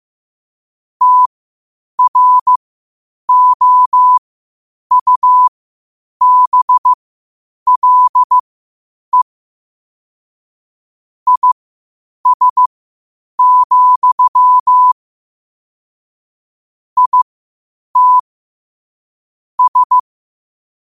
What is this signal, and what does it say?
Morse code, heard as a keyed tone, about 15 wpm, spaced as for 8 wpm, 1000 Hz. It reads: TROUBLE IS, IT S
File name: qotdfiles / 8wpm